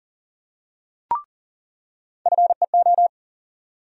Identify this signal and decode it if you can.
Morse code, heard as a keyed tone, about 40 wpm, 700 Hz. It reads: FEO
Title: USA Call Sign Postfixes - 3 Letters 40wpm